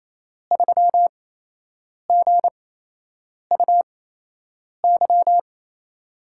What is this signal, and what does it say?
Morse code, heard as a keyed tone, about 28 wpm, spaced as for 8 wpm, 700 Hz. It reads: 3GUY
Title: USA Call Signs - 5 Characters - 8x Character Spacing - 28wpm